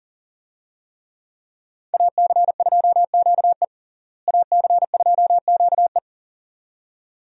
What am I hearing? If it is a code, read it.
Morse code, heard as a keyed tone, about 40 wpm, 700 Hz. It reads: AC2QE AC2QE
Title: USA Call Signs - 5 Characters - Repeated 2x - 40wpm